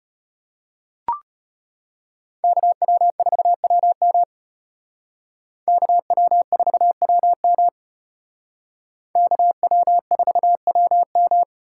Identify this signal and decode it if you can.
Morse code, heard as a keyed tone, about 33 wpm, 700 Hz. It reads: KW4WM KW4WM KW4WM